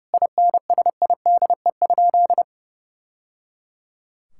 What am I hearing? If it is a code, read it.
Morse code, heard as a keyed tone, about 30 wpm, 700 Hz. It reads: INSIDE?
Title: Sentences from Top 1000 Words - Review Part 6 - 30wpm